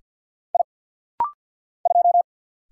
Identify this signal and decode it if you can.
Morse code, heard as a keyed tone, about 50 wpm, 690 Hz. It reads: I 2
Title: Single Letter-Number Alt - Rapid-Fire 50wpm